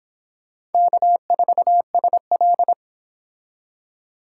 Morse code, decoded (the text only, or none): K4SL